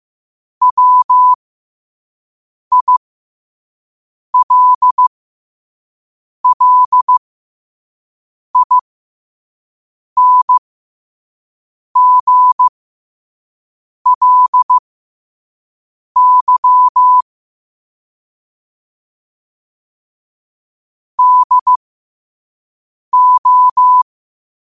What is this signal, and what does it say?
Morse code, heard as a keyed tone, about 15 wpm, spaced as for 5 wpm, 1000 Hz. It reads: WILLINGLY DO